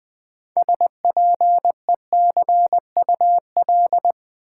Morse code, heard as a keyed tone, about 20 wpm, 700 Hz. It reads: SPECUL
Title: Build A Word - 3 Letters At A Time - 9-Letter Words - 20wpm